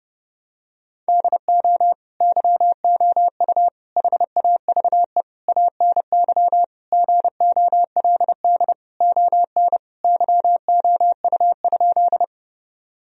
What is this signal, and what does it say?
Morse code, heard as a keyed tone, about 30 wpm, 700 Hz. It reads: DO YOU HAVE ANY GOLD ON YOU?
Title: Sentences from Top 1000 Words - Review Part 3 - 30wpm